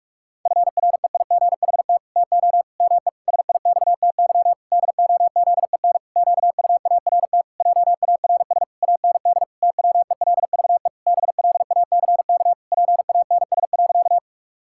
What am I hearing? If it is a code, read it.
Morse code, heard as a keyed tone, about 42 wpm, 700 Hz. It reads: FREIGHT TO ME SIXTY DOZEN QUART JARS AND TWELVE BLACK PANS.
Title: Pangram Challenge v2 - 45wpm